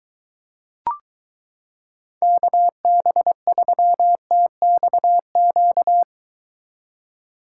Morse code, heard as a keyed tone, about 22 wpm, 700 Hz. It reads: KB3TXQ